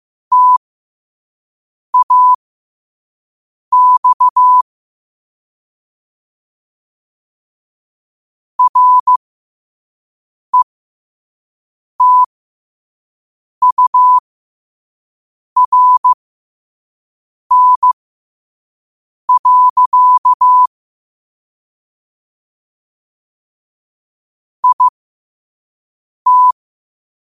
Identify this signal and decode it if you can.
Morse code, heard as a keyed tone, about 15 wpm, spaced as for 5 wpm, 1000 Hz. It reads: TAX RETURN. IT
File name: qotdfiles / 5wpm